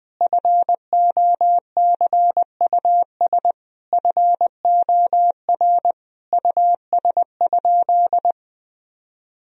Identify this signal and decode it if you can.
Morse code, heard as a keyed tone, about 20 wpm, 700 Hz. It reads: FOCUS FOR US?
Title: Sentences from Top 1000 Words - Review Part 4 - 20wpm